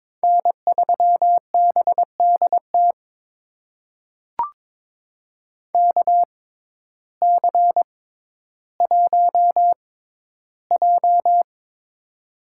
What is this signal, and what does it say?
Morse code, heard as a keyed tone, about 22 wpm, spaced as for 12 wpm, 700 Hz. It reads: N3BDT KC1J
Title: USA Call Signs - 6x Character Spacing - 22wpm